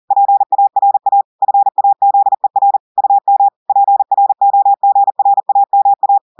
Morse code, heard as a keyed tone, about 40 wpm, 800 Hz. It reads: PARA FAZER UM PROGRAMA
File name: audios / portugais / portugais texte en clair 2 vitesse 40 mots minutes